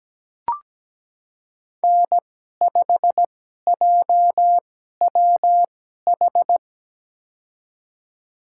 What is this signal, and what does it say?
Morse code, heard as a keyed tone, about 17 wpm, spaced as for 12 wpm, 700 Hz. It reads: N5JWH
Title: USA Call Signs - 5 Characters - 2x Character Spacing - 17wpm